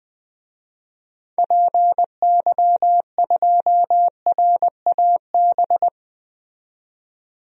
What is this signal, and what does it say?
Morse code, heard as a keyed tone, about 20 wpm, 700 Hz. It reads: PY2RAB
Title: DXCC-100 Call Signs - 6 Characters - 20wpm